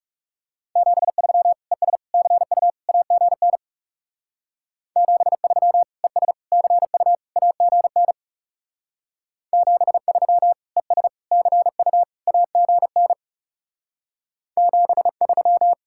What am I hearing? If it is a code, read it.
Morse code, heard as a keyed tone, about 40 wpm, 700 Hz. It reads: 73 ES CU AGN 73 ES CU AGN 73 ES CU AGN 73